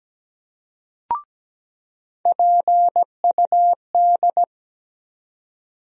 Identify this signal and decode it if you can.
Morse code, heard as a keyed tone, about 17 wpm, 700 Hz. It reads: PUD
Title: USA Call Sign Postfixes - 3 Letters 17wpm